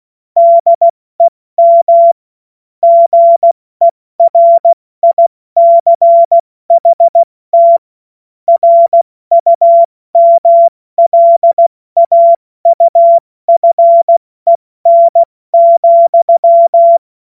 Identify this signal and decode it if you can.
Morse code, heard as a keyed tone, about 16 wpm, spaced as for 14 wpm, 680 Hz. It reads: DEM GERICHT RUMLAUFEN,
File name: Morsekurs / Witze / Witz 20. September 2022 70 F80